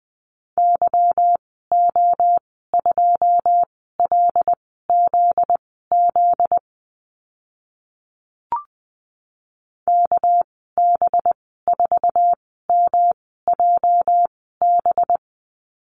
Morse code, heard as a keyed tone, about 20 wpm, spaced as for 14 wpm, 700 Hz. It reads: YO2LZZ KB4MJB
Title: 245 Lesson 39g - Letter z - Call Signs 20wpm